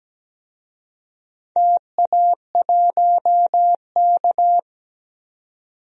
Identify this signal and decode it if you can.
Morse code, heard as a keyed tone, about 17 wpm, 700 Hz. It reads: TA1K
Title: DXCC-100 Call Signs - 4 Characters - 17wpm